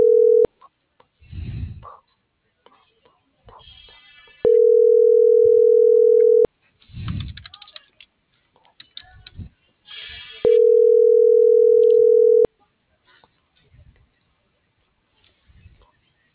Ringback tone (North America), 440 + 480 Hz.